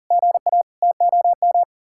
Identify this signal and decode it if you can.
Morse code, heard as a keyed tone, about 49 wpm, 700 Hz. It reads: GA TOM
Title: Speed-Racing Common QSO Phrases - 40wpm